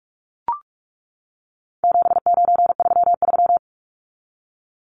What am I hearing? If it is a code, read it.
Morse code, heard as a keyed tone, about 44 wpm, 700 Hz. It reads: 7933
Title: Numbers - 4-Digit 45wpm